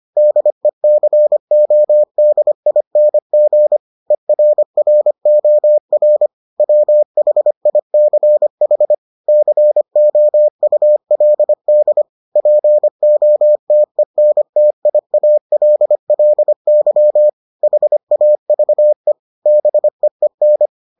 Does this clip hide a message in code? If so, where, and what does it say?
Yes: Morse code from 0.2 to 20.7 s, heard as a keyed tone, about 25 wpm, 600 Hz. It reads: DECODING ERROR WHICH COULD POTENTIALLY HAVE BEEN